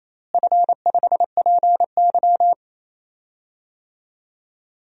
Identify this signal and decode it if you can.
Morse code, heard as a keyed tone, about 28 wpm, 700 Hz. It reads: F5PY
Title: DXCC-100 Call Signs - 4 Characters - Repeated 3x - 28wpm